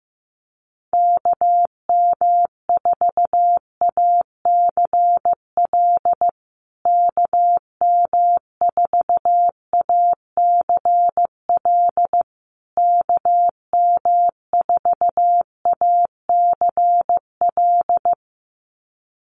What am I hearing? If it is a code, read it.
Morse code, heard as a keyed tone, about 15 wpm, 700 Hz. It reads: KM4ACL KM4ACL KM4ACL
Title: USA Call Signs Repeated 3x 15wpm